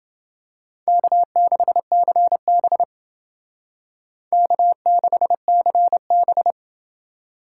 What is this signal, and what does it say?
Morse code, heard as a keyed tone, about 28 wpm, 700 Hz. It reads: K6CB K6CB